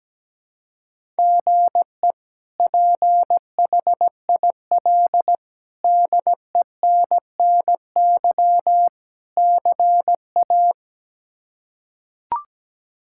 Morse code, heard as a keyed tone, about 17 wpm, 700 Hz. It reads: GE PHIL DENNY CA
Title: K1USN SST QSO Practice - 17wpm